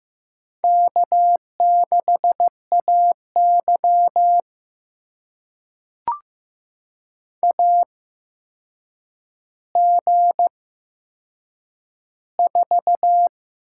Morse code, heard as a keyed tone, about 15 wpm, spaced as for 7 wpm, 700 Hz. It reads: K6AY AG4
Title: USA Call Signs - 8x Character Spacing - 15wpm